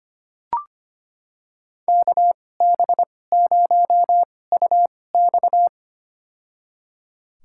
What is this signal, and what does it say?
Morse code, heard as a keyed tone, about 25 wpm, spaced as for 18 wpm, 700 Hz. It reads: KB0UX
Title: USA Call Signs - 5 Characters - 2x Character Spacing - 25wpm